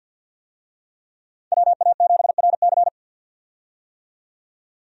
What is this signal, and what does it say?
Morse code, heard as a keyed tone, about 52 wpm, 700 Hz. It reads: WA7RC
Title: USA Call Signs - 5 Characters - 50wpm